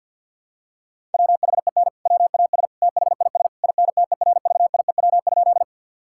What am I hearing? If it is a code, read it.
Morse code, heard as a keyed tone, about 50 wpm, 700 Hz. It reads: WHEN WAS THIS INTERVIEW?
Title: Sentences from Top 1600 Words 50wpm